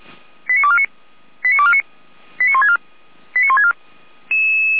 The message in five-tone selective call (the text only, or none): 89249 89249 89164 89164